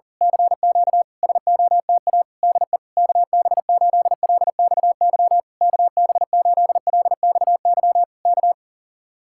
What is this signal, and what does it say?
Morse code, heard as a keyed tone, about 39 wpm, 700 Hz. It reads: CQ SOTA DE KB8LXY KB8LXY K
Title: Speed-Racing Common QSO Phrases Encore - 40wpm